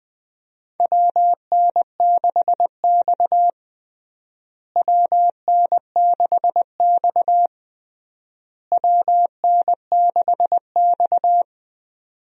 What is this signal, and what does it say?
Morse code, heard as a keyed tone, about 21 wpm, 700 Hz. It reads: WN6X WN6X WN6X
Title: USA Call Signs - 4 Characters - Repeated 3x - 20wpm